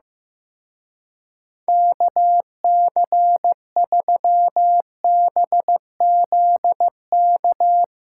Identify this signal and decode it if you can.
Morse code, heard as a keyed tone, about 15 wpm, 700 Hz. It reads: KC3BZK
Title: Speed Racing - USA 2x3 Call Signs 15wpm